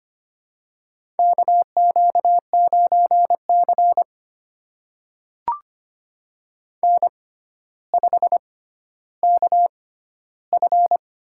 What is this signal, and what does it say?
Morse code, heard as a keyed tone, about 25 wpm, spaced as for 13 wpm, 700 Hz. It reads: KQ9C N5KF
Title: USA Call Signs - 6x Character Spacing - 25wpm